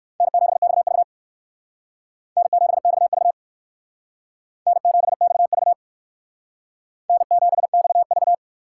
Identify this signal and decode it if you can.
Morse code, heard as a keyed tone, about 56 wpm, 700 Hz. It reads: N7XV N7XV N7XV N7XV